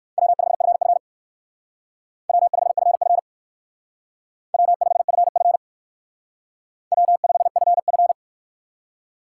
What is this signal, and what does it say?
Morse code, heard as a keyed tone, about 57 wpm, 700 Hz. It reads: W5FF W5FF W5FF W5FF